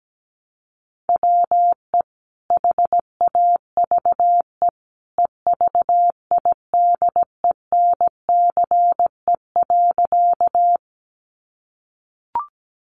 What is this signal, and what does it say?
Morse code, heard as a keyed tone, about 17 wpm, 700 Hz. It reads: WE HAVE EVIDENCE.
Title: Sentences from Top 1300 Words 17wpm